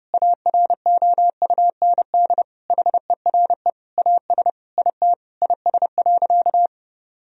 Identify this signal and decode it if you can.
Morse code, heard as a keyed tone, about 31 wpm, 700 Hz. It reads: AROUND HERE AS IT IS.
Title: Sentences from Top 1000 Words - Review Part 3 - 30wpm